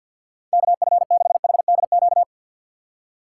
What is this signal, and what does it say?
Morse code, heard as a keyed tone, about 60 wpm, 690 Hz. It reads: KF6HDQ
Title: Speed Racing - USA 2x3 Call Signs 50wpm